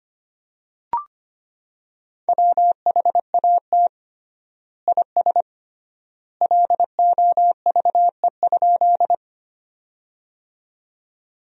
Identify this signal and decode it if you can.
Morse code, heard as a keyed tone, about 25 wpm, 700 Hz. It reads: WHAT IS LOVE?
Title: 3x Word-Spacing - Sentences from Top 400 Words - 25wpm